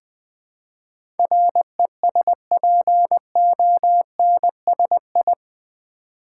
Morse code, heard as a keyed tone, about 20 wpm, 700 Hz. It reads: RESPONSI
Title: Build A Word - 4 Letters At A Time - 11-Letter Words - 20wpm